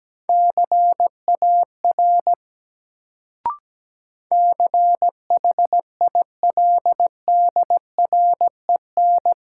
Morse code, heard as a keyed tone, about 17 wpm, 700 Hz. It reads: CAR CHILDREN
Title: Sentences from Top 300 Words - New Word Drill - 17wpm